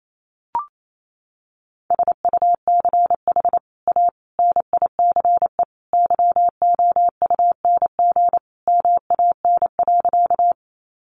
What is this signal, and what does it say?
Morse code, heard as a keyed tone, about 28 wpm, 700 Hz. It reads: SUCH A NICE YOUNG MAN.
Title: Sentences from Top 1100 Words 28wpm